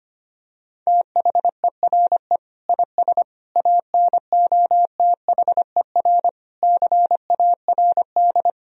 Morse code, heard as a keyed tone, about 25 wpm, 700 Hz. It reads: THERE IS ANOTHER CARD
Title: Sentences from Top 1000 Words - Review Part 4 - 25wpm